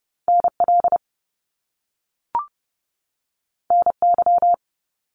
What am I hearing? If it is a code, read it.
Morse code, heard as a keyed tone, about 30 wpm, 700 Hz. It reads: NL NY
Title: Top Two Letter Combinations in Words 30wpm